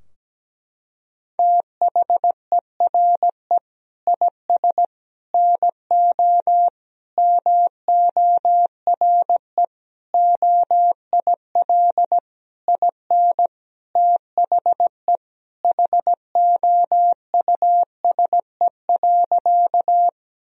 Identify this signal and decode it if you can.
Morse code, heard as a keyed tone, about 17 wpm, 700 Hz. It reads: THERE IS NO MORE OIL IN THE HOUSE.
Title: Sentences from Top 1000 Words - Review Part 4 - 17wpm